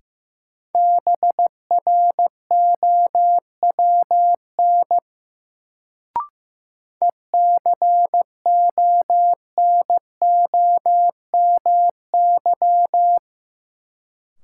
Morse code, heard as a keyed tone, about 15 wpm, 700 Hz. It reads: BROWN ECONOMY
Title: Sentences from Top 1200 Words - New Word Drill Alt - 15wpm